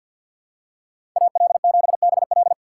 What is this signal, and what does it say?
Morse code, heard as a keyed tone, about 52 wpm, 700 Hz. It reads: AL7BL
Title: USA Call Signs - 5 Characters - 3x Character Spacing - 50wpm